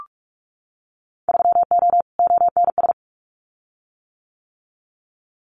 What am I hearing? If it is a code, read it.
Morse code, heard as a keyed tone, about 46 wpm, 690 Hz. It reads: 3O ONS
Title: Winter Field Day Exchanges Encore 45wpm